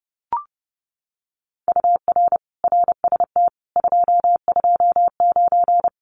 Morse code, heard as a keyed tone, about 30 wpm, 700 Hz. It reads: UR RST 229
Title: Common QSO Phrases Encore 30wpm